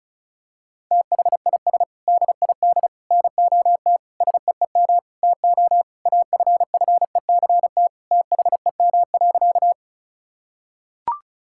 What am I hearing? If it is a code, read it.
Morse code, heard as a keyed tone, about 36 wpm, 700 Hz. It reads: THIS DID NOT SEEM TO AFFECT THEM.
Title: Sentences from Top 1000 Words - Review Part 6 - 35wpm